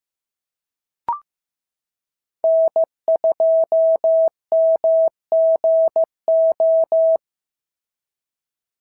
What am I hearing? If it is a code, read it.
Morse code, heard as a keyed tone, about 14 wpm, 650 Hz. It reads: N2MGO
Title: SOTA QSO Practice 15wpm